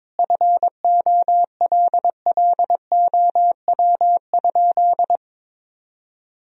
Morse code, heard as a keyed tone, about 22 wpm, 700 Hz. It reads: FOLLOW?
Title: Sentences from Top 1000 Words - Review Part 3 - 22wpm